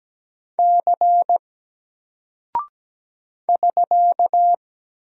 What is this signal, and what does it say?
Morse code, heard as a keyed tone, about 17 wpm, 700 Hz. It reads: C <SK>
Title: Single Character (Letter:Number:Punctuation:Prosign) Alt 17wpm